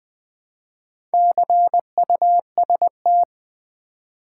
Morse code, heard as a keyed tone, about 20 wpm, 700 Hz. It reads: CUST
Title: Build A Word - 2 Letters At A Time - 6-Letter Words - 20wpm